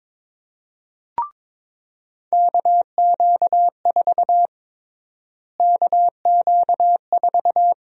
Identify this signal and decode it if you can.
Morse code, heard as a keyed tone, about 22 wpm, 700 Hz. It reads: KQ4 KQ4
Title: USA Call Sign Prefixes - Repeated 2x - 22wpm